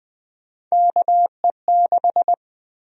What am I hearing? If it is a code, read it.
Morse code, heard as a keyed tone, about 20 wpm, 700 Hz. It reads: KE6
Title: USA Call Sign Prefixes - Repeated 2x - 20wpm